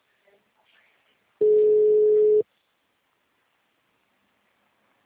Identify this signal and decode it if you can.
Ringback tone (Europe, 425 Hz).